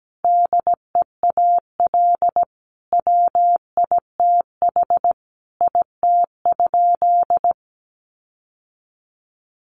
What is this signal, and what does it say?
Morse code, heard as a keyed tone, about 17 wpm, 700 Hz. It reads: DEAL WITH IT?
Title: Sentences from Top 1400 Words 17wpm